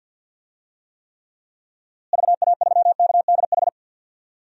Morse code, heard as a keyed tone, about 49 wpm, 700 Hz. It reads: VA3KDH
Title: DXCC-100 Call Signs - 6 Characters - Repeated 2x - 50wpm